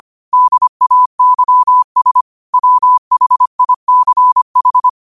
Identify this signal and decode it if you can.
Morse code, heard as a keyed tone, about 25 wpm, 1000 Hz. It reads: DAYS WHICH
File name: qotdfiles / 25wpm